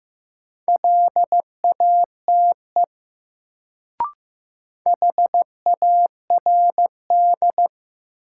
Morse code, heard as a keyed tone, about 15 wpm, 700 Hz. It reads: LATE HARD
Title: Sentences from Top 300 Words - New Word Drill - 15wpm